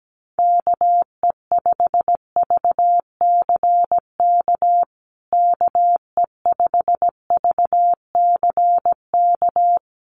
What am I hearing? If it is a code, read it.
Morse code, heard as a keyed tone, about 17 wpm, 700 Hz. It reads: KE5VCK KE5VCK
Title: USA Call Signs Repeated 2x 17wpm